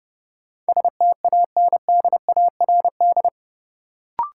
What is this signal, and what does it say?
Morse code, heard as a keyed tone, about 32 wpm, 700 Hz. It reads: STANDARD
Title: Sentences from Top 1200 Words - New Word Drill Alt - 30wpm